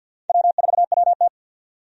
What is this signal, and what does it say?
Morse code, heard as a keyed tone, about 54 wpm, 700 Hz. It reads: W4WT
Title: USA Call Signs - 4 Characters - 50wpm